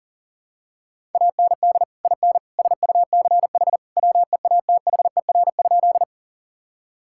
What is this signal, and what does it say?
Morse code, heard as a keyed tone, about 40 wpm, 700 Hz. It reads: AND IN SUCH WEATHER?
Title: Sentences from Top 1000 Words - Review Part 5 - 40wpm